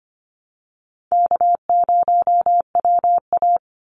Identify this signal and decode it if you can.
Morse code, heard as a keyed tone, about 25 wpm, 700 Hz. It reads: K0WA
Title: Easy Call Signs Repeated 2x - 25wpm